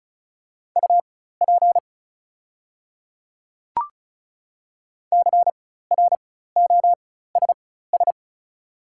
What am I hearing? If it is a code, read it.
Morse code, heard as a keyed tone, about 34 wpm, spaced as for 16 wpm, 700 Hz. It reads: UP CROSS
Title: 4x Character Spacing - Top 500 Words - 35wpm